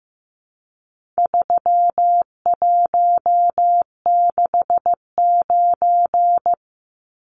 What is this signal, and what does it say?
Morse code, heard as a keyed tone, about 15 wpm, 700 Hz. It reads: 3169
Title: Numbers - 4-Digit 15wpm